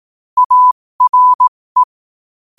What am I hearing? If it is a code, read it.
Morse code, heard as a keyed tone, about 18 wpm, spaced as for 15 wpm, 1000 Hz. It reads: ARE
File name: qotdfiles / 15wpm